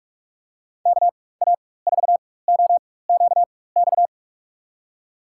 Morse code, heard as a keyed tone, about 47 wpm, spaced as for 21 wpm, 700 Hz. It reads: KA4OQX
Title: USA Call Signs - 6 Characters - 4x Character Spacing - 45wpm